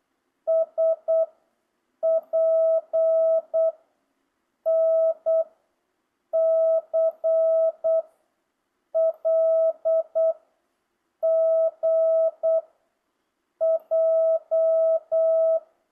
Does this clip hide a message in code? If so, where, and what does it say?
Yes: Morse code from 0.5 to 15.6 s, heard as a keyed tone, about 8 wpm, 650 Hz. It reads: S P N C L G J